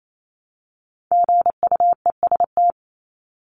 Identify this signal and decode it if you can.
Morse code, heard as a keyed tone, about 28 wpm, 700 Hz. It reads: GUEST
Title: Top 1501-1750 Words 28wpm